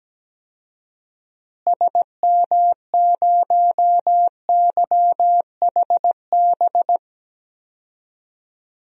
Morse code, heard as a keyed tone, about 17 wpm, 700 Hz. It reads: SM0YHB
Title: DXCC-100 Call Signs - 6 Characters - Repeated 2x - 17wpm